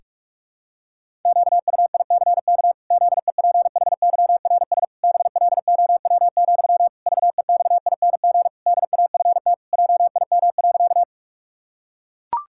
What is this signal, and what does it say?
Morse code, heard as a keyed tone, about 44 wpm, 700 Hz. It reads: QUICK ZEPHYRS BLOW, VEXING DAFT JIM.
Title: Pangram Challenge v2 - 45wpm